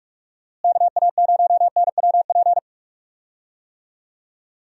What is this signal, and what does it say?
Morse code, heard as a keyed tone, about 46 wpm, 700 Hz. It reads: KA0NWP
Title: USA Call Signs - 6 Characters - Repeated 2x - 45wpm